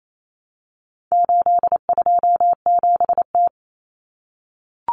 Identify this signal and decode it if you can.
Morse code, heard as a keyed tone, about 28 wpm, 700 Hz. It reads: 827T
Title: Common Cut Numbers - 4-Digit 28wpm